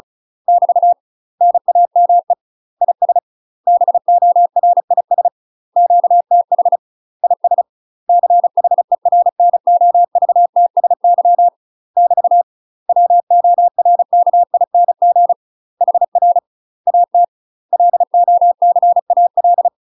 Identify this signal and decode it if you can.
Morse code, heard as a keyed tone, about 36 wpm, 700 Hz. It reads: <BT> NAME IS BORIS QTH IS CHERNOVTSY <BT> WORKING HR AT LOCAL
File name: UK MASTER QSOs TRAINING / 35WPM / UK MASTER QSOs 0041